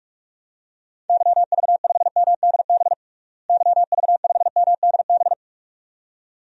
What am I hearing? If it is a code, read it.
Morse code, heard as a keyed tone, about 46 wpm, 700 Hz. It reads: YV5MDB YV5MDB
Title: DXCC-100 Call Signs - 6 Characters - Repeated 2x - 45wpm